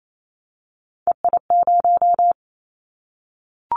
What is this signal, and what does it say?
Morse code, heard as a keyed tone, about 28 wpm, 700 Hz. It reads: EI0
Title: Callsign Prefixes for Top 50 DXCC - 28wpm